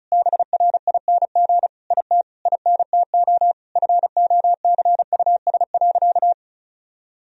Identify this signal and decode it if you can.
Morse code, heard as a keyed tone, about 34 wpm, 700 Hz. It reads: BRING IT INTO FOCUS.
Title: Sentences from Top 1000 Words - Review Part 4 - 35wpm